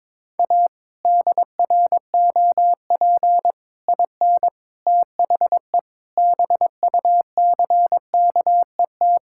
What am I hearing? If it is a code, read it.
Morse code, heard as a keyed tone, about 22 wpm, 700 Hz. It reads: A DROP IN THE BUCKET
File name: practice / 22wpm / Sets of 8 Words - 22wpm